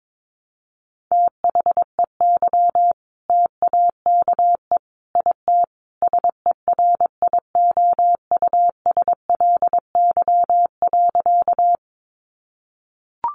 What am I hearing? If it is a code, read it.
Morse code, heard as a keyed tone, about 22 wpm, 700 Hz. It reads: THEY TAKE IT SERIOUSLY.
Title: Sentences from Top 1200 Words 22wpm